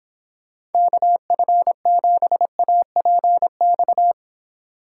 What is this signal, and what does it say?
Morse code, heard as a keyed tone, about 29 wpm, 700 Hz. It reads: KF7APX